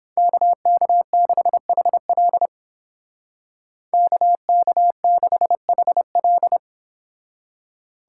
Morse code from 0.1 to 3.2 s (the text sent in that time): KK6HL